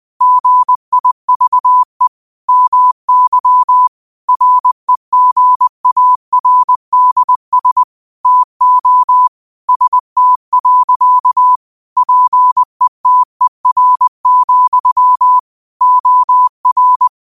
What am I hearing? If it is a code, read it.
Morse code, heard as a keyed tone, about 20 wpm, 1000 Hz. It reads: GIVE MY REGARDS TO ST. PETER, OR
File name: qotdfiles / 20wpm